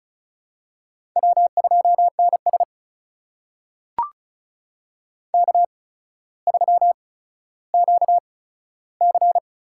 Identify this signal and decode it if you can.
Morse code, heard as a keyed tone, about 36 wpm, spaced as for 15 wpm, 700 Hz. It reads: W2NS K3QC
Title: USA Call Signs - 8x Character Spacing - 35wpm